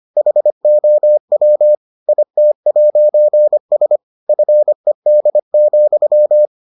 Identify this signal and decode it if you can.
Morse code, heard as a keyed tone, about 25 wpm, 600 Hz. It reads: HOW IT'S FED,